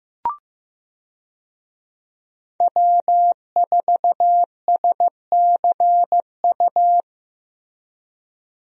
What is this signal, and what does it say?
Morse code, heard as a keyed tone, about 15 wpm, 700 Hz. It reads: W4SCU
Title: USA Call Signs - 5 Characters - Repeated 2x - 15wpm